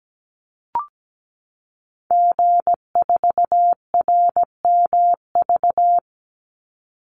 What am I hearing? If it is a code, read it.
Morse code, heard as a keyed tone, about 17 wpm, 700 Hz. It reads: G4RMV
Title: Call Signs Repeated 3x - 17wpm